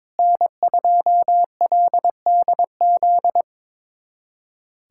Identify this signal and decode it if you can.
Morse code, heard as a keyed tone, about 22 wpm, 700 Hz. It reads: N2LDZ
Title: USA Callsign DE Callsign - 5 Characters - 22wpm